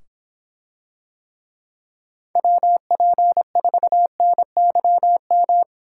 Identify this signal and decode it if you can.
Morse code, heard as a keyed tone, about 24 wpm, 700 Hz. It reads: WP4NYM